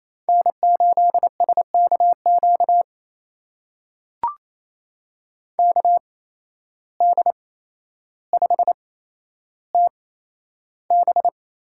Morse code, heard as a keyed tone, about 28 wpm, spaced as for 12 wpm, 700 Hz. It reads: N8SKQ KD5TB
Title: USA Call Signs - 5 Characters - 8x Character Spacing - 28wpm